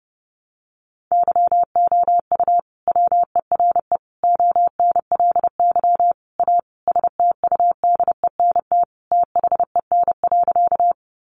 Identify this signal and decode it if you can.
Morse code, heard as a keyed tone, about 28 wpm, 700 Hz. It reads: YOU WERE ONLY A STUDENT THEN.
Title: Sentences from Top 1000 Words 30wpm